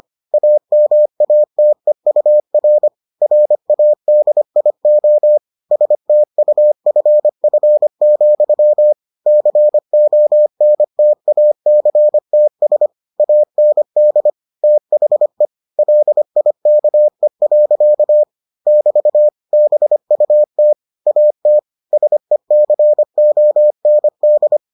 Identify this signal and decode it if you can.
Morse code, heard as a keyed tone, about 25 wpm, 600 Hz. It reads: AMATEUR RADIO STUFF, CONTACTS AND THE LIKE. <BT> BUT AT SECOND